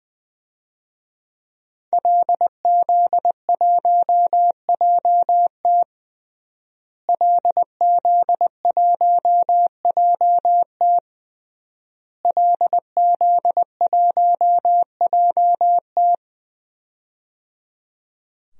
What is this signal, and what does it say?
Morse code, heard as a keyed tone, about 20 wpm, 700 Hz. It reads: LZ1JT LZ1JT LZ1JT
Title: DXCC-100 Call Signs - 5 Characters - Repeated 3x - 20wpm